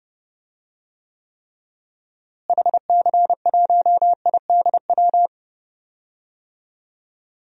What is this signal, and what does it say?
Morse code, heard as a keyed tone, about 30 wpm, 700 Hz. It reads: HC1IDW
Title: DXCC-100 Call Signs - 6 Characters - Repeated 3x - 30wpm